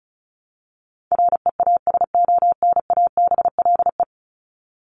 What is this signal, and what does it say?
Morse code, heard as a keyed tone, about 34 wpm, 700 Hz. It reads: REASONABLE
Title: Top 1501-1750 Words 35wpm